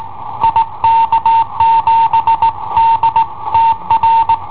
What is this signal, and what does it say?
Morse code, heard as a keyed tone, about 20 wpm, 910 Hz. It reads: IK7DTR
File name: i7ohp / cassetta i7ohp B